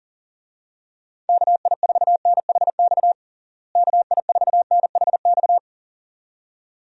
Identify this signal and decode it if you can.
Morse code, heard as a keyed tone, about 40 wpm, 700 Hz. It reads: KI4NHX KI4NHX
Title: USA Call Signs - 6 Characters - Repeated 2x - 40wpm